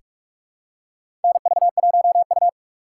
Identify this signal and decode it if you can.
Morse code, heard as a keyed tone, about 46 wpm, 700 Hz. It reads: NV1U
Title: K1USN SST QSO Practice - 45wpm